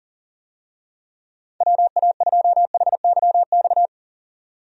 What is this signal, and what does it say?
Morse code, heard as a keyed tone, about 40 wpm, 700 Hz. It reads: WA2HYX
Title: USA Call Signs - 6 Characters - 40wpm